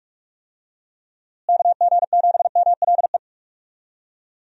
Morse code, heard as a keyed tone, about 46 wpm, 700 Hz. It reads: KG7MLE
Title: USA Call Signs - 6 Characters - Repeated 2x - 45wpm